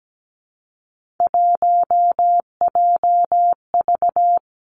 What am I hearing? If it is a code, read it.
Morse code, heard as a keyed tone, about 17 wpm, 700 Hz. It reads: 1JV
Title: Three Challenging Letter-Number Combinations 17wpm